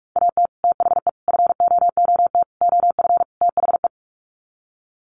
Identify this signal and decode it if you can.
Morse code, heard as a keyed tone, about 44 wpm, 700 Hz. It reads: AT THE FOOT OF THE
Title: Sets of 5 Words 45wpm